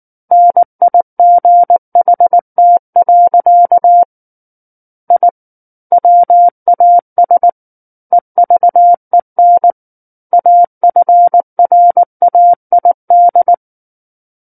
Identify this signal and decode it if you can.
Morse code, heard as a keyed tone, about 19 wpm, 700 Hz. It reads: NIGHT. I WAS EVEN AFRAID
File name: Book One / 10-Chapter 10 - 19 WPM